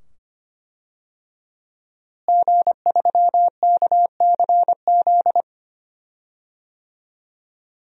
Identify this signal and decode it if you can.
Morse code, heard as a keyed tone, about 25 wpm, 700 Hz. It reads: G3KCZ
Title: DXCC-100 Call Signs - 5 Characters - Repeated 2x - 25wpm